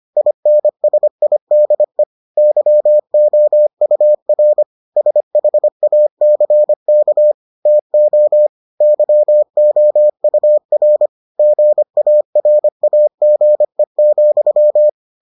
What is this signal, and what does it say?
Morse code, heard as a keyed tone, about 25 wpm, 600 Hz. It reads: INSIDE YOUR SHACK TO YOUR GARAGE,